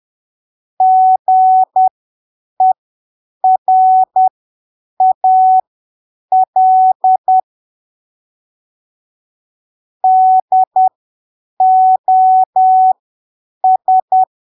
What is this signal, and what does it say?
Morse code, heard as a keyed tone, about 10 wpm, spaced as for 7 wpm, 750 Hz. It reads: GERAL DOS
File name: arquivos / texto / texto 14 10ppm